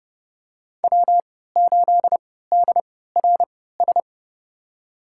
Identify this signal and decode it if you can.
Morse code, heard as a keyed tone, about 32 wpm, spaced as for 17 wpm, 700 Hz. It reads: W8DRS
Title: USA Call Signs - 3x Character Spacing - 30wpm